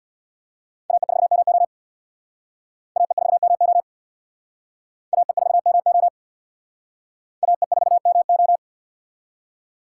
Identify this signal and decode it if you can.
Morse code, heard as a keyed tone, about 62 wpm, 700 Hz. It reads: AE4MO AE4MO AE4MO AE4MO
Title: Speed Racing - USA 2x2 Call Signs 50wpm